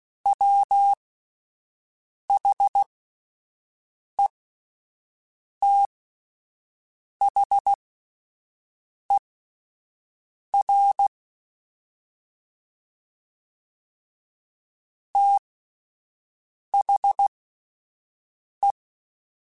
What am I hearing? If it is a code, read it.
Morse code, heard as a keyed tone, about 16 wpm, spaced as for 5 wpm, 800 Hz. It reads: WHETHER THE
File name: Spaced / Chuckles Spaced-mp3s / HalfFull sp @10